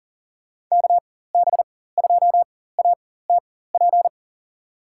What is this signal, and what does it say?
Morse code, heard as a keyed tone, about 40 wpm, spaced as for 19 wpm, 700 Hz. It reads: KB2ATP
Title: USA Call Signs - 6 Characters - 4x Character Spacing - 40wpm